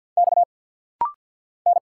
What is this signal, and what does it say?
Morse code, heard as a keyed tone, about 50 wpm, 700 Hz. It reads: X N
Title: Single Letters - Rapid-Fire 50wpm